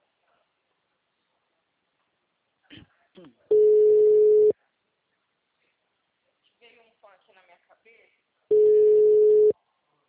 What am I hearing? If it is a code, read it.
Ringback tone (Europe, 425 Hz), with speech.
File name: out- -2002- - - .6